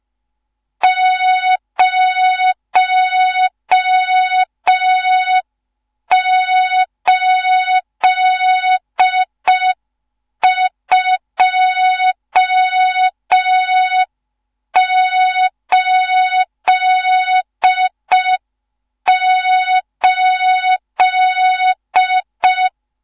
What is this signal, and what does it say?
Morse code, heard as a keyed tone, about 5 wpm, 3700 Hz. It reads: 08288
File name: Morze-HA2MN / hang / 14 280 HA2MN